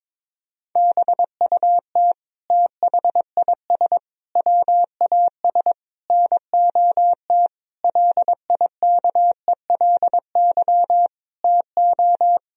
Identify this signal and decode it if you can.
Morse code, heard as a keyed tone, about 22 wpm, 700 Hz. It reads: BUT THIS WAS NOT LIKELY TO
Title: Sentences from Top 1000 Words - Review Part 4 - 22wpm